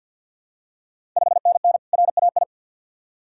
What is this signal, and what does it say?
Morse code, heard as a keyed tone, about 52 wpm, 690 Hz. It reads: 5NN RAI
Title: Ontario QSO Party Exchange 50wpm